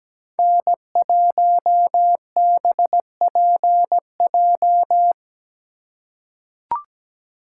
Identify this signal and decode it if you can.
Morse code, heard as a keyed tone, about 17 wpm, 700 Hz. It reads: N1BPJ
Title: USA Call Signs - 5 Characters - 4x Character Spacing - 17wpm